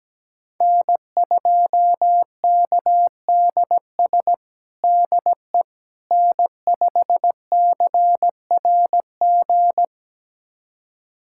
Morse code, heard as a keyed tone, about 17 wpm, 700 Hz. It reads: N2KDS DE N5CRG
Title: USA Callsign DE Callsign - 5 Characters - 17wpm